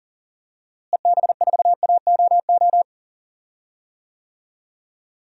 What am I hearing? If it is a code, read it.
Morse code, heard as a keyed tone, about 40 wpm, 700 Hz. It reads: EB4AOO
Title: DXCC-100 Call Signs - 6 Characters - Repeated 3x - 40wpm